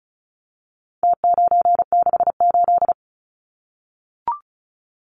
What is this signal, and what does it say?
Morse code, heard as a keyed tone, about 34 wpm, 700 Hz. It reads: T968
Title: Numbers - Contest Style - 4-Digit 35wpm